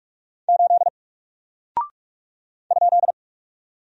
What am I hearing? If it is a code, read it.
Morse code, heard as a keyed tone, about 44 wpm, 700 Hz. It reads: 8 ?
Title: Single Character (Letter:Number:Punctuation:Prosign) 45wpm